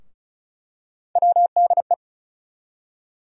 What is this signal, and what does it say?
Morse code, heard as a keyed tone, about 36 wpm, 700 Hz. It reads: WDE
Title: USA Call Sign Postfixes - 3 Letters 35wpm